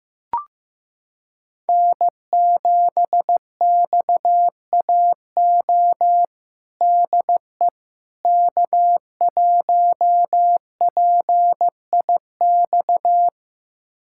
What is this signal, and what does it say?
Morse code, heard as a keyed tone, about 15 wpm, 700 Hz. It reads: N7XAO DE K1PIX
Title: USA Callsign DE Callsign - 5 Characters - 15wpm